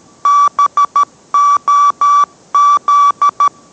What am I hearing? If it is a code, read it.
Morse code, heard as a keyed tone, about 13 wpm, 1150 Hz. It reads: BOZ